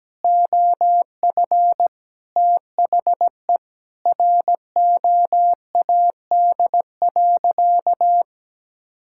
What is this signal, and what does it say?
Morse code, heard as a keyed tone, about 17 wpm, 700 Hz. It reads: OF THE ROAD.
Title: Sentences from Top 1000 Words - Review Part 5 - 17wpm